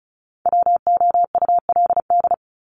Morse code, heard as a keyed tone, about 34 wpm, 700 Hz. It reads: WOULD
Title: Top 200 Words 35wpm